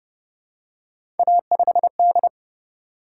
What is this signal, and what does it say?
Morse code, heard as a keyed tone, about 28 wpm, 700 Hz. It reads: A5D